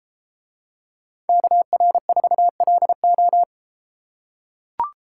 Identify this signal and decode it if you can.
Morse code, heard as a keyed tone, about 33 wpm, 700 Hz. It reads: KR4LO